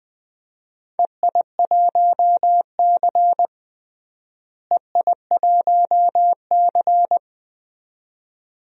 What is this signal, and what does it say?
Morse code, heard as a keyed tone, about 20 wpm, 700 Hz. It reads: EI1C EI1C
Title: DXCC-100 Call Signs - 4 Characters - Repeated 2x - 20wpm